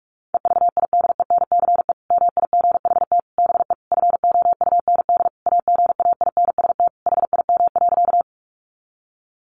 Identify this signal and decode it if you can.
Morse code, heard as a keyed tone, about 44 wpm, 700 Hz. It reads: EVIDENCE MIGHT BE FOUND AGAINST HIM.
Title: Sentences from Top 1300 Words 45wpm